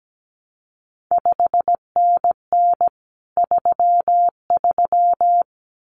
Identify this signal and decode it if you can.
Morse code, heard as a keyed tone, about 17 wpm, 700 Hz. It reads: 5NN 33
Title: CQ WW DX Contest Exchange Encore - 17wpm